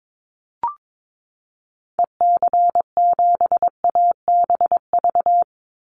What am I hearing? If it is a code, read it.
Morse code, heard as a keyed tone, about 22 wpm, 700 Hz. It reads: EC7ABV
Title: Call Signs - 22wpm